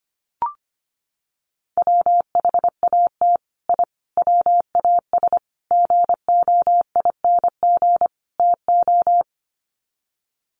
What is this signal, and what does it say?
Morse code, heard as a keyed tone, about 25 wpm, 700 Hz. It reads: WHAT I WAS GOING TO
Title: Sets of 5 Words Encore2 25wpm